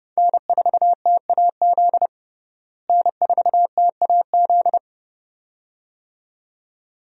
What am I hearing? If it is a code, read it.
Morse code, heard as a keyed tone, about 30 wpm, 700 Hz. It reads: N4TAZ N4TAZ
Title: USA Call Signs - 5 Characters - Repeated 2x - 30wpm